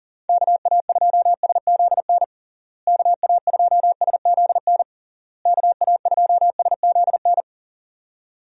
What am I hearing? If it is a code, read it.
Morse code, heard as a keyed tone, about 40 wpm, 700 Hz. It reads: KA2SZN KA2SZN KA2SZN
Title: USA Call Signs - 6 Characters - Repeated 3x - 40wpm